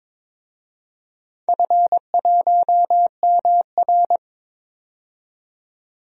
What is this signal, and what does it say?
Morse code, heard as a keyed tone, about 22 wpm, 700 Hz. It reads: F1MR
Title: DXCC-100 Call Signs - 4 Characters - 22wpm